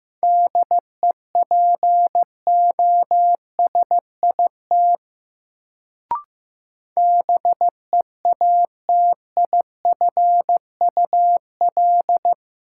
Morse code, heard as a keyed tone, about 15 wpm, 700 Hz. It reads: DEPOSIT BEATIFUL
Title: Sentences from Top 1300 Words - New Word Drill - 15wpm